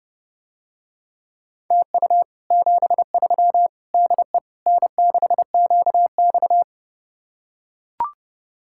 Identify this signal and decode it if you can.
Morse code, heard as a keyed tone, about 30 wpm, 700 Hz. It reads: TU 73 DE N6QX